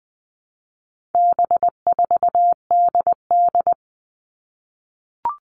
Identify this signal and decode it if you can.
Morse code, heard as a keyed tone, about 20 wpm, 700 Hz. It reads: B4DD
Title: All Cut Numbers - 4-Digit 20wpm